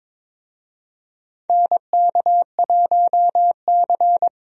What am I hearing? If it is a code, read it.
Morse code, heard as a keyed tone, about 22 wpm, 700 Hz. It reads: NK1C
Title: USA Call Signs - 4 Characters - Repeated 2x - 22wpm